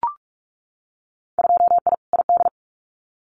Morse code, heard as a keyed tone, about 46 wpm, 690 Hz. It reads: 2I ID
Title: Winter Field Day Exchanges Encore 45wpm